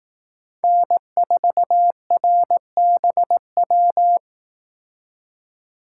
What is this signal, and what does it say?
Morse code, heard as a keyed tone, about 18 wpm, 700 Hz. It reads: N4RBW
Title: Speed Racing - USA 1x3 Call Signs 15wpm